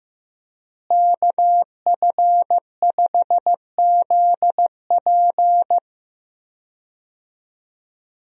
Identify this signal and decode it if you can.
Morse code, heard as a keyed tone, about 16 wpm, 700 Hz. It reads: KF5ZP
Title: USA Call Signs - 5 Characters - Repeated 3x - 15wpm